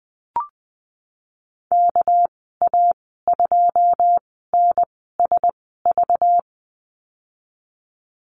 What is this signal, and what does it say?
Morse code, heard as a keyed tone, about 20 wpm, spaced as for 15 wpm, 700 Hz. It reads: KA2NSV
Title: 207 Lesson 33g - Letter j - Call Signs 20wpm